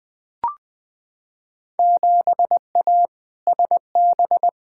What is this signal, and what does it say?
Morse code, heard as a keyed tone, about 20 wpm, 700 Hz. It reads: 7A SB
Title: Field Day Exchanges Encore - 20wpm